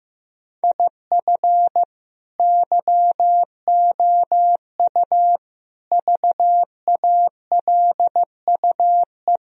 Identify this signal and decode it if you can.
Morse code, heard as a keyed tone, about 15 wpm, 700 Hz. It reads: IF YOU VALUE
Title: Sentences from Top 1000 Words - Review Part 3 - 15wpm